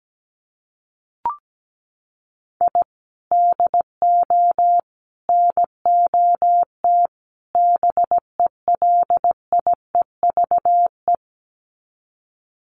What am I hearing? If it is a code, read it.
Morse code, heard as a keyed tone, about 17 wpm, 700 Hz. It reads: I DO NOT BELIEVE
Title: Sets of 4 Words 17wpm